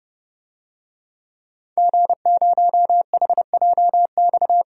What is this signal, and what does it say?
Morse code, heard as a keyed tone, about 30 wpm, 700 Hz. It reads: G0HJX
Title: DXCC-100 Call Signs - 5 Characters - Repeated 2x - 30wpm